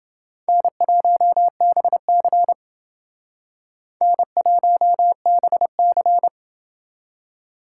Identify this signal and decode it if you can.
Morse code, heard as a keyed tone, about 30 wpm, 700 Hz. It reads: N1BC N1BC